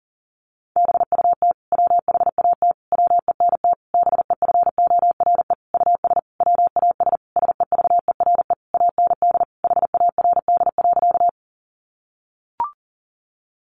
Morse code, heard as a keyed tone, about 40 wpm, 700 Hz. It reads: BUT WHAT WENT BEFORE US WAS SEVERE AND HARD.
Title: Sentences from Top 1400 Words 40wpm